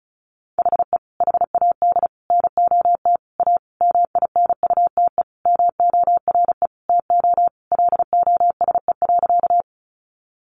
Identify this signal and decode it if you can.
Morse code, heard as a keyed tone, about 34 wpm, 700 Hz. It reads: HE HAD NOT A MINUTE MORE TO LOSE.
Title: Sentences from Top 1100 Words 35wpm